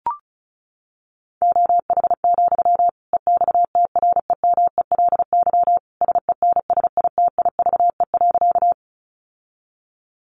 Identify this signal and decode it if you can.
Morse code, heard as a keyed tone, about 34 wpm, 700 Hz. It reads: OH, EXTREMELY SENSITIVE.
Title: Sentences from Top 1400 Words 35wpm